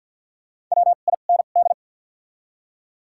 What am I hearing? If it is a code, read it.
Morse code, heard as a keyed tone, about 52 wpm, spaced as for 37 wpm, 700 Hz. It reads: WIND
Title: 2x Character Spacing - Top 500 Words - 50wpm